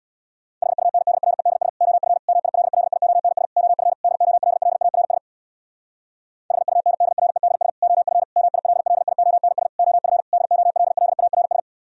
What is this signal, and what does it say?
Morse code, heard as a keyed tone, about 64 wpm, 700 Hz. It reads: HUNDREDS OF MILLIONS OF DOLLARS HUNDREDS OF MILLIONS OF DOLLARS